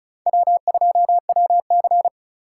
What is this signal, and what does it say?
Morse code, heard as a keyed tone, about 35 wpm, 700 Hz. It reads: W2WC
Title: USA Call Signs - 4 Characters - 4x Character Spacing - 35wpm